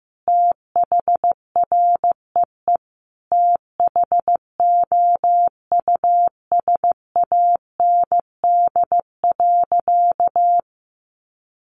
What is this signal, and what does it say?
Morse code, heard as a keyed tone, about 15 wpm, 700 Hz. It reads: THREE THOUSAND.
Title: Sentences from Top 1300 Words 15wpm